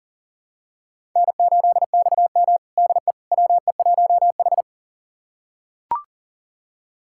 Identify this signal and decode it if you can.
Morse code, heard as a keyed tone, about 40 wpm, 700 Hz. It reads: N8XM DE WE1H
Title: USA Callsign DE Callsign - 4 Characters - 40wpm